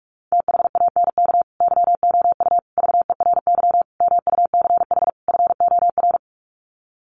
Morse code, heard as a keyed tone, about 44 wpm, 700 Hz. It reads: THANK YOU VERY MUCH FOR
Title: Sets of 5 Words 45wpm